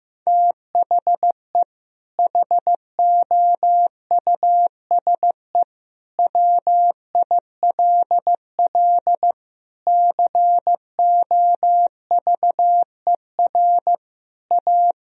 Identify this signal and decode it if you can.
Morse code, heard as a keyed tone, about 16 wpm, 700 Hz. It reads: THE HOUSE WILL COVER A
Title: Sentences from Top 1000 Words - Review Part 2 - 15wpm